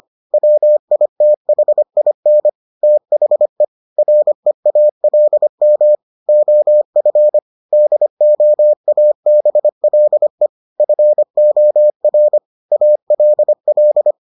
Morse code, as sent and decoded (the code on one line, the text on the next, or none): .-- .. - .... .. -. / - .... / .-. .- .-.. -- / --- ..-. / -.. --- .- -... .-.. / ..-. --- .-. / .- .-.. .-..
WITHIN THE REALM OF DOABLE FOR ALL